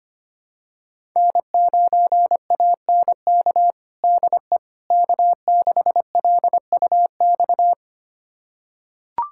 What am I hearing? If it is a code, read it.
Morse code, heard as a keyed tone, about 25 wpm, 700 Hz. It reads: N9ANK DE K6LUX
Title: USA Callsign DE Callsign 25wpm